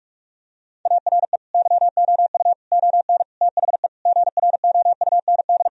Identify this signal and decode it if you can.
Morse code, heard as a keyed tone, about 46 wpm, 700 Hz. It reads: ARE YOU ON THE GROUND
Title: Sentences from Top 1000 Words - Review Part 2 - 45wpm